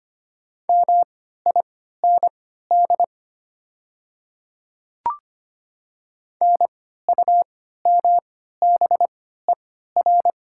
Morse code, heard as a keyed tone, about 25 wpm, spaced as for 14 wpm, 700 Hz. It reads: MIND NUMBER
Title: 3x Character Spacing - Top 500 Words - 25wpm